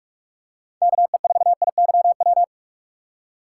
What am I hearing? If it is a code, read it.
Morse code, heard as a keyed tone, about 46 wpm, 700 Hz. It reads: KE4IYW
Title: USA Call Signs - 6 Characters - 6x Character Spacing - 45wpm